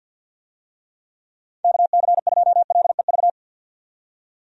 Morse code, heard as a keyed tone, about 52 wpm, 700 Hz. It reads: KC2LEV
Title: USA Call Signs - 6 Characters - 50wpm